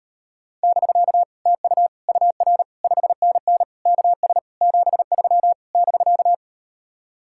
Morse code, heard as a keyed tone, about 38 wpm, 700 Hz. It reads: <BK> TU UR 5NN KS 73 <BK>